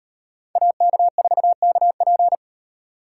Morse code, heard as a keyed tone, about 38 wpm, 700 Hz. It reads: AK4KP